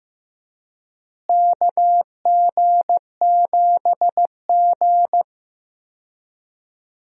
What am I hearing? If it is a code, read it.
Morse code, heard as a keyed tone, about 16 wpm, 700 Hz. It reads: KG7G
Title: USA Call Signs - 4 Characters - Repeated 2x - 15wpm